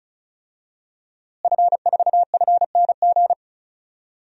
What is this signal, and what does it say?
Morse code, heard as a keyed tone, about 36 wpm, 700 Hz. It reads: F4FNG
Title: DXCC-100 Call Signs - 5 Characters - Repeated 2x - 35wpm